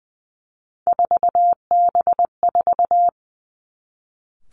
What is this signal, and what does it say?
Morse code, heard as a keyed tone, about 20 wpm, 700 Hz. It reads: 4B4
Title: Three Challenging Letter-Number Combinations 20wpm